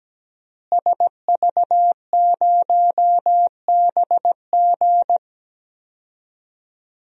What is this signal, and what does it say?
Morse code, heard as a keyed tone, about 17 wpm, 700 Hz. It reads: SV0BG
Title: DXCC-100 Call Signs - 5 Characters - Repeated 3x - 17wpm